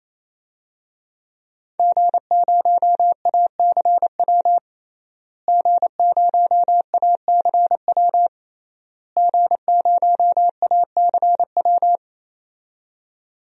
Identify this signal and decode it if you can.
Morse code, heard as a keyed tone, about 28 wpm, 700 Hz. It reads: G0ACW G0ACW G0ACW
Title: DXCC-100 Call Signs - 5 Characters - Repeated 3x - 28wpm